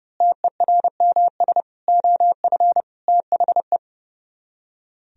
Morse code, heard as a keyed tone, about 29 wpm, 700 Hz. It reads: TERMS OF THE
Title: Speed-Racing - Sets of 4 Words 30wpm